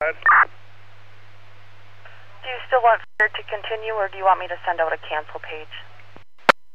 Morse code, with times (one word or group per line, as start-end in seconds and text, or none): none